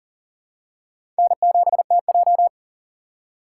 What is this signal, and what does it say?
Morse code, heard as a keyed tone, about 40 wpm, 700 Hz. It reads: N7TJ
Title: USA Call Signs - 6x Character Spacing - 40wpm